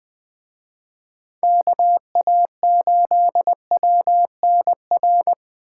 Morse code, heard as a keyed tone, about 24 wpm, 700 Hz. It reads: KA8WNR
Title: Speed Racing - USA 2x3 Call Signs 20wpm